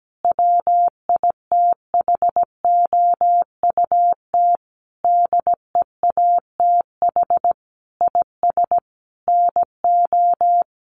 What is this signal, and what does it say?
Morse code, heard as a keyed tone, about 17 wpm, 700 Hz. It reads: WITHOUT DEATH IS NO
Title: Sentences from Top 1100 Words 17wpm